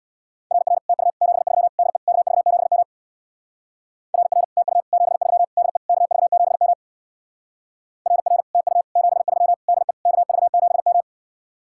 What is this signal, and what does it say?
Morse code, heard as a keyed tone, about 64 wpm, 700 Hz. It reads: RR TU 73 DE KV7M RR TU 73 DE KV7M RR TU 73 DE KV7M